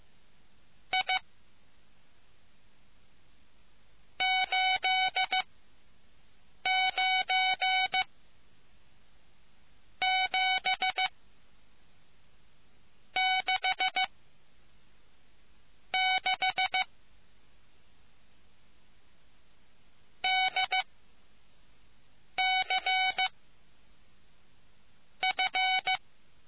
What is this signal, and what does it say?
Morse code, heard as a keyed tone, about 15 wpm, 750 Hz. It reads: I 8 9 7 6 6 D C F